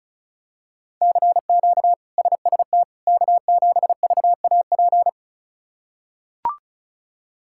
Morse code, heard as a keyed tone, about 36 wpm, 700 Hz. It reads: CQ SST K7VAP
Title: K1USN SST QSO Practice - 35wpm